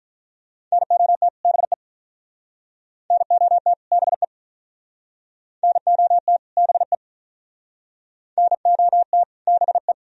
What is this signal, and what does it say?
Morse code, heard as a keyed tone, about 45 wpm, 700 Hz. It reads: NOT BE NOT BE NOT BE NOT BE